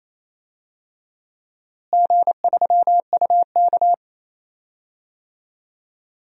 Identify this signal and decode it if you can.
Morse code, heard as a keyed tone, about 27 wpm, 700 Hz. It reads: G3UK
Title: DXCC-100 Call Signs - 4 Characters - Repeated 3x - 28wpm